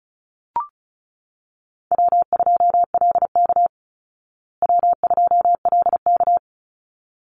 Morse code, heard as a keyed tone, about 34 wpm, 700 Hz. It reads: W2LK W2LK
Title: Easy Call Signs Repeated 2x - 35wpm